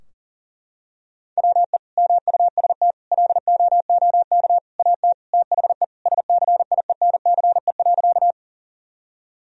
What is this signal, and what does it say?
Morse code, heard as a keyed tone, about 40 wpm, 700 Hz. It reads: WE MUST LOOK AT THE SCIENCE.
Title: Sentences from Top 1000 Words - Review Part 2 - 40wpm